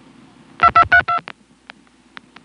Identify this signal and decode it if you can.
DTMF, heard as telephone touch tones, tones of about 90 ms. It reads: 2232